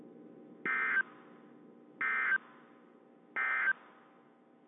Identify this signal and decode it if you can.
Emergency Alert System (SAME) tones, heard as harsh buzzing bursts. It is heard from 0.6 to 3.7 s.
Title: WZ2523 - NOAA Weather Radio